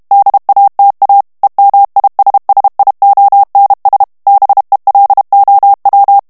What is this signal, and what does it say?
Morse code, heard as a keyed tone, about 32 wpm, 800 Hz. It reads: DATA EMISSIONS BELOW
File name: morse mp3 / ANGLAIS texte clair 3 vitesse 30 mots minute